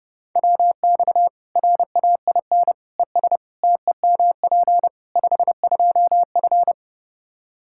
Morse code, heard as a keyed tone, about 30 wpm, 700 Hz. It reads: WX RAIN ES TEMP 52F
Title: Practice QSO - Exchange 2 - Info - 30wpm